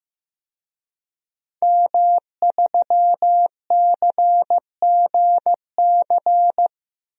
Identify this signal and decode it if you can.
Morse code, heard as a keyed tone, about 15 wpm, 700 Hz. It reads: M3CGC
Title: DXCC-100 Call Signs - 5 Characters - 15wpm